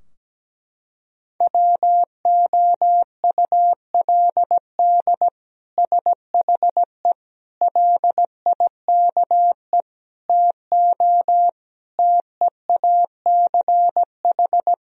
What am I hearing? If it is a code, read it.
Morse code, heard as a keyed tone, about 17 wpm, 700 Hz. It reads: WOULD SHE LIKE TO TEACH
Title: Sentences from Top 1000 Words - Review Part 2 - 17wpm